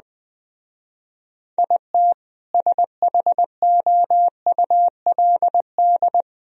Morse code, heard as a keyed tone, about 20 wpm, 700 Hz. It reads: IT SHOULD
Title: Speed-Racing - Sets of 5 Words 17wpm